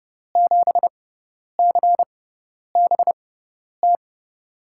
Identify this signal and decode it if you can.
Morse code, heard as a keyed tone, about 30 wpm, spaced as for 10 wpm, 700 Hz. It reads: 7CBT
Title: USA Call Signs - 6 Characters - 6x Character Spacing - 30wpm